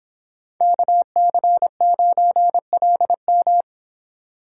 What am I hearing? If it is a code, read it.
Morse code, heard as a keyed tone, about 26 wpm, 700 Hz. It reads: KC9LM